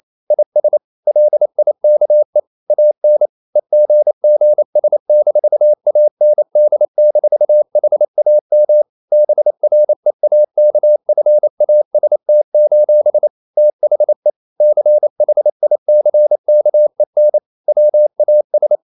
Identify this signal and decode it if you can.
Morse code, heard as a keyed tone, about 28 wpm, 600 Hz. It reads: IS LIKE AN EGGS-AND-HAM BREAKFAST: THE CHICKEN WAS
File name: CW / Practice / EnglishQuotations0128